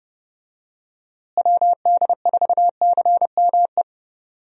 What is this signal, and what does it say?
Morse code, heard as a keyed tone, about 30 wpm, 700 Hz. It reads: WD4CME
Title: USA Call Signs - 6 Characters - Repeated 3x - 30wpm